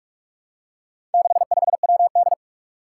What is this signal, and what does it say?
Morse code, heard as a keyed tone, about 46 wpm, 700 Hz. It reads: 65WD
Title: Four Challenging Letter-Number Combinations 45wpm